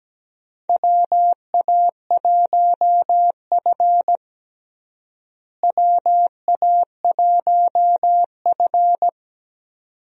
Morse code, heard as a keyed tone, about 17 wpm, 700 Hz. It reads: WA1F WA1F
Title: USA Call Signs - 4 Characters - Repeated 2x - 17wpm